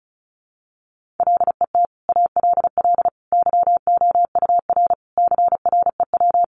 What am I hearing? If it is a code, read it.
Morse code, heard as a keyed tone, about 34 wpm, 700 Hz. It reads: LET ALL YOUR CREW
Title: Sentences from Top 1500 Words 35wpm